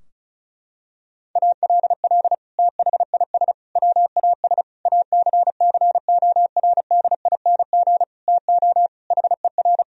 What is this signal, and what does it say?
Morse code, heard as a keyed tone, about 34 wpm, 700 Hz. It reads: ALL THIS WAS ACCORDING TO HER
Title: Sentences from Top 1000 Words - Review Part 4 - 35wpm